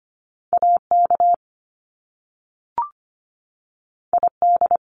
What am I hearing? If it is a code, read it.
Morse code, heard as a keyed tone, about 25 wpm, 700 Hz. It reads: AK ID
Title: US States 25wpm